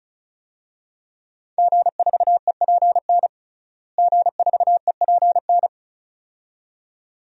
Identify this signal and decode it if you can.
Morse code, heard as a keyed tone, about 36 wpm, 700 Hz. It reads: G4EPN G4EPN
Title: DXCC-100 Call Signs - 5 Characters - Repeated 2x - 35wpm